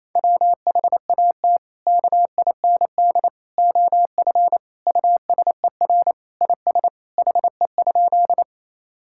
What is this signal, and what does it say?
Morse code, heard as a keyed tone, about 28 wpm, 700 Hz. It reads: WHAT KIND OF USER IS HE?
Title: Sentences from Top 1000 Words - Review Part 5 - 28wpm